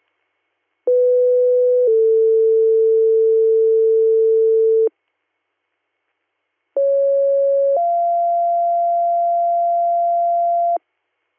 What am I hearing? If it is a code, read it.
Two-tone page, heard as a one-second tone, then a three-second tone, about 500 then 450 Hz, about 550 then 690 Hz. It is heard from 0.9 to 10.8 s.